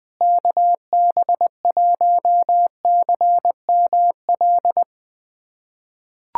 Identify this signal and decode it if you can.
Morse code, heard as a keyed tone, about 20 wpm, 700 Hz. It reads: KB1CML
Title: USA Callsign DE Callsign 20wpm